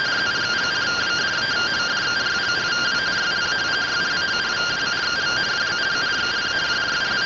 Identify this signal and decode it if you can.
Two-tone teleprinter signal, about 170 Hz shift.